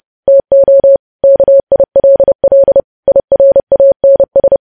Morse code, heard as a keyed tone, about 30 wpm, 550 Hz. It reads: TO KILL IRANS
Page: news headlines in morse code at 30 wpm